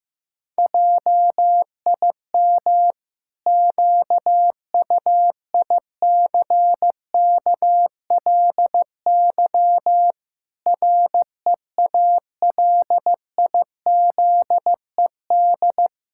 Morse code, heard as a keyed tone, about 15 wpm, 700 Hz. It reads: JIM QUICKLY REALIZED
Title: Pangram Challenge v2 - 15wpm